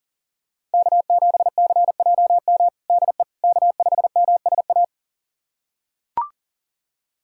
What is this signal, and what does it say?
Morse code, heard as a keyed tone, about 40 wpm, 700 Hz. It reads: K7CJM DE K5MSA
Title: USA Callsign DE Callsign - 5 Characters - 40wpm